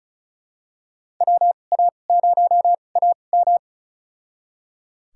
Morse code, heard as a keyed tone, about 36 wpm, spaced as for 25 wpm, 700 Hz. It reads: WA0AM
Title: USA Call Signs - 5 Characters - 2x Character Spacing - 35wpm